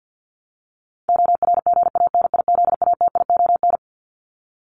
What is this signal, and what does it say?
Morse code, heard as a keyed tone, about 50 wpm, 700 Hz. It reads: ORGANIZATION
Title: Top 501-750 Words 50wpm